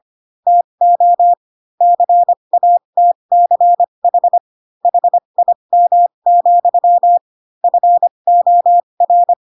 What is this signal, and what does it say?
Morse code, heard as a keyed tone, about 25 wpm, spaced as for 22 wpm, 700 Hz. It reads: TO CATCH HIM, FOR
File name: Pinocchio / Pinocchio ch3 18 25 ik1hge